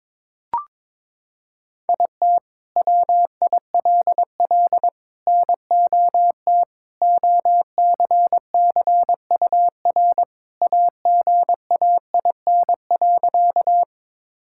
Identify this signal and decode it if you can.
Morse code, heard as a keyed tone, about 23 wpm, 700 Hz. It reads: IT WILL NOT OCCUR AGAIN.
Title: Sentences from Top 1000 Words - Review Part 6 - 22wpm